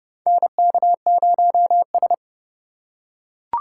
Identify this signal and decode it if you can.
Morse code, heard as a keyed tone, about 30 wpm, 700 Hz. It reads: NK0S
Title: USA Call Signs - 4 Characters - 3x Character Spacing - 30wpm